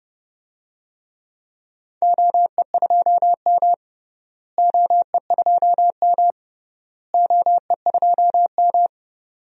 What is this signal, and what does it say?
Morse code, heard as a keyed tone, about 32 wpm, 700 Hz. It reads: OE2M OE2M OE2M
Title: DXCC-100 Call Signs - 4 Characters - Repeated 3x - 30wpm